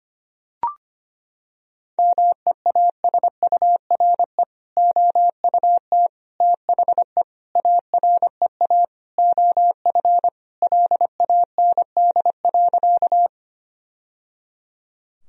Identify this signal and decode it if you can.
Morse code, heard as a keyed tone, about 25 wpm, 700 Hz. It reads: MEASURE OUT THE AREA OF LAND.
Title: Sentences from Top 1000 Words - Review Part 2 - 25wpm